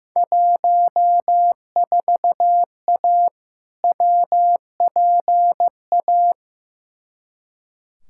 Morse code, heard as a keyed tone, about 16 wpm, 700 Hz. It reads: 14A WPA
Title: Field Day Exchanges Encore - 15wpm